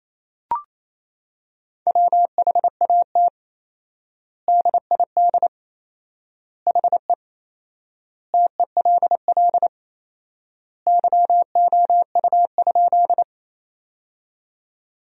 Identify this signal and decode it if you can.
Morse code, heard as a keyed tone, about 28 wpm, 700 Hz. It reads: WHAT DID HE TELL YOU?
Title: 4x Word-Spacing - Sentences from Top 200 Words - 28wpm